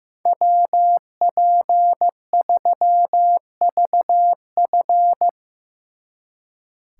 Morse code, heard as a keyed tone, about 16 wpm, 700 Hz. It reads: WP3VF
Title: Speed Racing - USA 2x2 Call Signs 15wpm